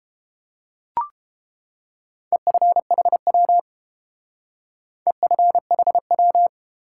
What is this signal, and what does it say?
Morse code, heard as a keyed tone, about 30 wpm, 700 Hz. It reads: EFHW EFHW